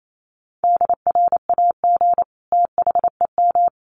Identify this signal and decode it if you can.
Morse code, heard as a keyed tone, about 28 wpm, 700 Hz. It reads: DRAG THEM
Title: Sentences from Top 1500 Words 28wpm